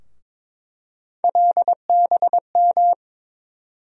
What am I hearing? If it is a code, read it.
Morse code, heard as a keyed tone, about 22 wpm, 700 Hz. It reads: LBM
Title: USA Call Sign Postfixes - 3 Letters 22wpm